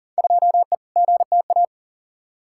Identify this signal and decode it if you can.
Morse code, heard as a keyed tone, about 38 wpm, 700 Hz. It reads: 2E GTA
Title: Field Day Exchanges Encore - 40wpm